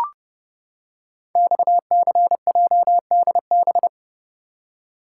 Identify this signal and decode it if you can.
Morse code, heard as a keyed tone, about 30 wpm, 700 Hz. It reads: XCJDB
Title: Code Groups - 5 Letters - 30wpm